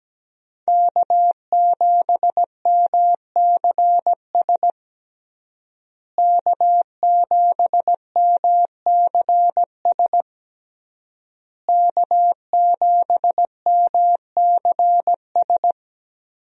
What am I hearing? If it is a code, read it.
Morse code, heard as a keyed tone, about 17 wpm, 700 Hz. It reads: K7MCS K7MCS K7MCS
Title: USA Call Signs - 5 Characters - Repeated 3x - 17wpm